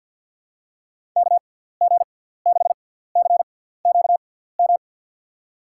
Morse code, heard as a keyed tone, about 52 wpm, spaced as for 17 wpm, 700 Hz. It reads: KG6CQM
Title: USA Call Signs - 6 Characters - 6x Character Spacing - 50wpm